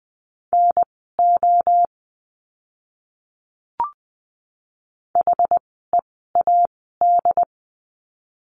Morse code, heard as a keyed tone, about 20 wpm, spaced as for 14 wpm, 700 Hz. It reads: NO HEAD
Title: 068 Lesson 12f - Letter d - Words 20wpm